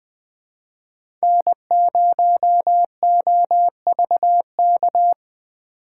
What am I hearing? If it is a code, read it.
Morse code, heard as a keyed tone, about 20 wpm, 700 Hz. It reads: N0OVK
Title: USA Call Signs - 5 Characters - 4x Character Spacing - 20wpm